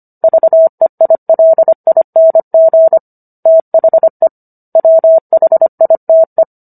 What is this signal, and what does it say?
Morse code, heard as a keyed tone, about 25 wpm, 650 Hz. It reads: VEILING THE WHITE